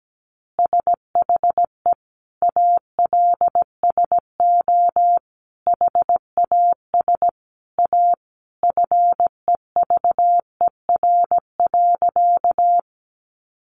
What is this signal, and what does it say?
Morse code, heard as a keyed tone, about 17 wpm, 700 Hz. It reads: SHE ALSO HAS A FEVER.
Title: Easy Phrases 17wpm